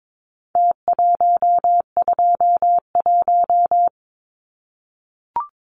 Morse code, heard as a keyed tone, about 22 wpm, 700 Hz. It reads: T121
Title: Numbers - Contest Style - 4-Digit 22wpm